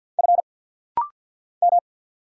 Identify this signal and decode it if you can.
Morse code, heard as a keyed tone, about 50 wpm, 700 Hz. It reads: F M
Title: Single Letters - Rapid-Fire 50wpm